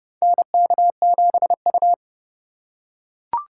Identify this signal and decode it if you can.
Morse code, heard as a keyed tone, about 30 wpm, 700 Hz. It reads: NK7U
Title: K1USN SST QSO Practice - 30wpm